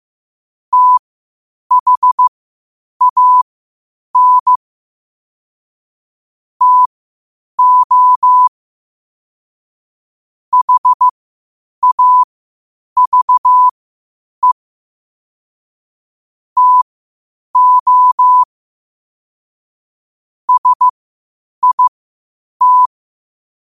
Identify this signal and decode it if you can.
Morse code, heard as a keyed tone, about 15 wpm, spaced as for 8 wpm, 1000 Hz. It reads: THAN TO HAVE TO SIT
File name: qotdfiles / 8wpm